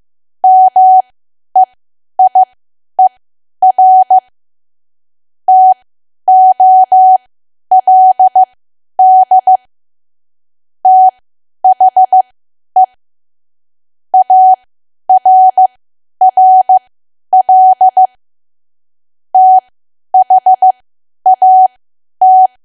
Morse code, heard as a keyed tone, about 15 wpm, spaced as for 10 wpm, 750 Hz. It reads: MEIER TOLD THE ARRL THAT
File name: cwa / W1AW-Level2 / 10WPM